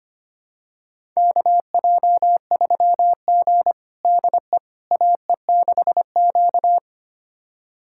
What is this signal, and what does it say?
Morse code, heard as a keyed tone, about 25 wpm, 700 Hz. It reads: KJ3G DE AE6Q
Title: USA Callsign DE Callsign - 4 Characters - 25wpm